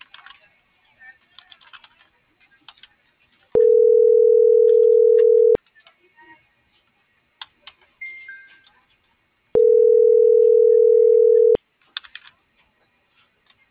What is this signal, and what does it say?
Ringback tone (North America), 440 + 480 Hz.